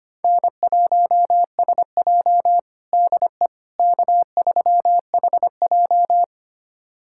Morse code, heard as a keyed tone, about 25 wpm, 700 Hz. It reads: N1SJ DE K3HJ
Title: USA Callsign DE Callsign - 4 Characters - 25wpm